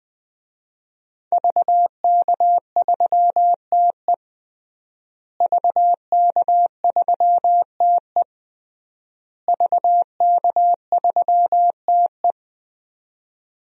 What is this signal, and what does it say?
Morse code, heard as a keyed tone, about 20 wpm, 700 Hz. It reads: VK3TE VK3TE VK3TE
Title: DXCC-100 Call Signs - 5 Characters - Repeated 3x - 20wpm